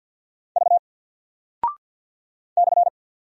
Morse code, heard as a keyed tone, about 48 wpm, 700 Hz. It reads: V /
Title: Single Character (Letter:Number:Punctuation:Prosign) Alt 50wpm